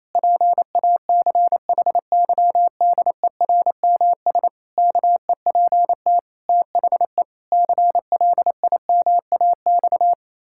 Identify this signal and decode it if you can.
Morse code, heard as a keyed tone, about 28 wpm, 700 Hz. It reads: PACHYDERMS KEPT THE CLIMAX
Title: Pangram Challenge v2 - 28wpm